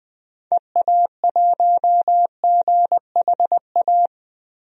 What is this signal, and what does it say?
Morse code, heard as a keyed tone, about 20 wpm, 700 Hz. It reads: EA1GHA
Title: DXCC-100 Call Signs - 6 Characters - Repeated 3x - 20wpm